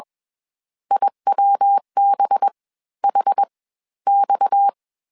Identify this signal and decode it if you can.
Morse code, heard as a keyed tone, about 21 wpm, spaced as for 14 wpm, 800 Hz. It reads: IWBHX